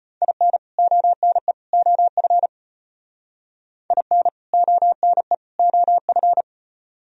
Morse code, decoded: IN ONE OF IN ONE OF

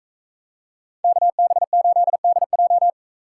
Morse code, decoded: KB8DJ

